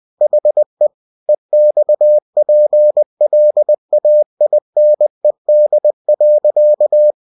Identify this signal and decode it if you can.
Morse code, heard as a keyed tone, about 20 wpm, 600 Hz. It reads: HE EXPLAINED.
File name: cw-books / winnie-the-pooh / winnie-20wpm-page-0047